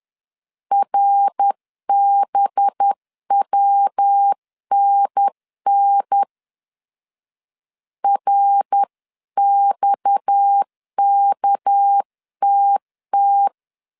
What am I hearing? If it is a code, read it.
Morse code, heard as a keyed tone, about 10 wpm, 800 Hz. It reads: RBWNN RXKTT
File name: CW / DET / 1 6 1500C S C 10wpm 4252